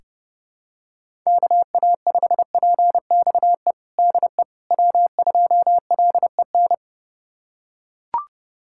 Morse code, heard as a keyed tone, about 30 wpm, 700 Hz. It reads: KA5PXE DE W2LEN
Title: USA Callsign DE Callsign 30wpm